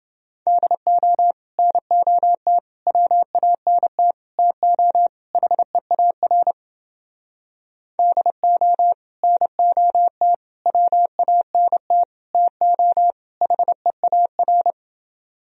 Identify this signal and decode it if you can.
Morse code, heard as a keyed tone, about 30 wpm, 700 Hz. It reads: DO NOT WANT TO HEAR DO NOT WANT TO HEAR